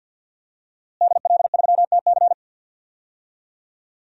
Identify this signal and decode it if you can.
Morse code, heard as a keyed tone, about 52 wpm, 700 Hz. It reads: DL3TC
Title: DXCC-100 Call Signs - 5 Characters - Repeated 3x - 50wpm